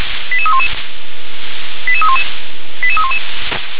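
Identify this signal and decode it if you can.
Five-tone selective call (ZVEI), heard as quick runs of five tones, tones of about 70 ms. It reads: 88311 88311 88311
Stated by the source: ZVEI five-tone selective call received on an SDR.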